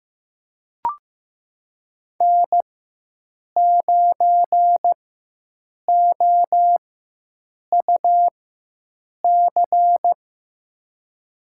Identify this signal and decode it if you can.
Morse code, heard as a keyed tone, about 16 wpm, spaced as for 7 wpm, 700 Hz. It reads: N9OUC
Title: USA Call Signs - 5 Characters - 4x Character Spacing - 15wpm